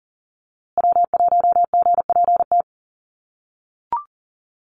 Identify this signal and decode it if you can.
Morse code, heard as a keyed tone, about 38 wpm, 700 Hz. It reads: W1GPT
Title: USA Call Signs Repeated 3x 40wpm